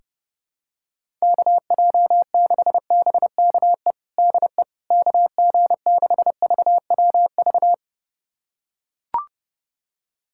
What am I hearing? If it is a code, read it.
Morse code, heard as a keyed tone, about 30 wpm, 700 Hz. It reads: KJ6BKE DE KG6VWV
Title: USA Callsign DE Callsign - 6 Characters - 30wpm